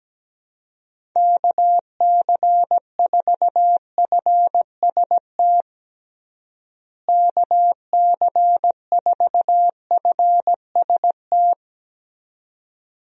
Morse code, heard as a keyed tone, about 17 wpm, 700 Hz. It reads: KC4FST KC4FST
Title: USA Call Signs - 6 Characters - Repeated 2x - 17wpm